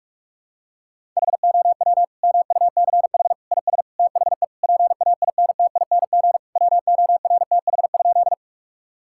Morse code, heard as a keyed tone, about 46 wpm, 700 Hz. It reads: HOW MUCH IS THE PAINTING WORTH?
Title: Sentences from Top 1000 Words - Review Part 6 - 45wpm